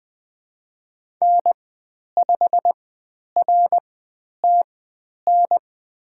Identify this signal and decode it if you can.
Morse code, heard as a keyed tone, about 20 wpm, spaced as for 10 wpm, 700 Hz. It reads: N5RTN
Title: 102 Lesson 17g - Letter u - Call Signs 20wpm